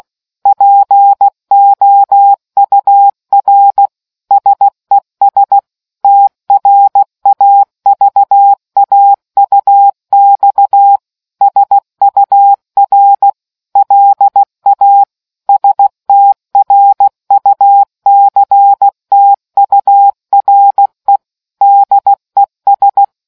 Morse code, heard as a keyed tone, about 16 wpm, 800 Hz. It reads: POUR SES TRAVAUX SUR LA STRUCTURE DES